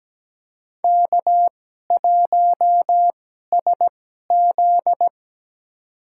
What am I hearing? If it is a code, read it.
Morse code, heard as a keyed tone, about 17 wpm, spaced as for 12 wpm, 700 Hz. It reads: K1SZ
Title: USA Call Signs - 2x Character Spacing - 17wpm